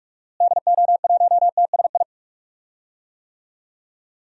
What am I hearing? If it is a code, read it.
Morse code, heard as a keyed tone, about 43 wpm, 700 Hz. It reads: DO1TSI
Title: DXCC-100 Call Signs - 6 Characters - Repeated 3x - 45wpm